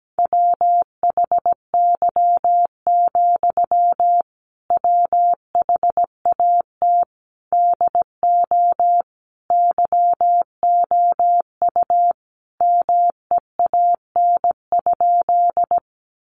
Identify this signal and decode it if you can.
Morse code, heard as a keyed tone, about 17 wpm, 700 Hz. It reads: WHY, WHAT DO YOU MEAN?
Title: Sentences from Top 200 Words 17wpm